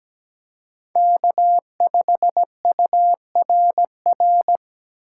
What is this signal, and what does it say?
Morse code, heard as a keyed tone, about 17 wpm, 700 Hz. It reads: K5URR
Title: USA Call Signs - 5 Characters - 2x Character Spacing - 17wpm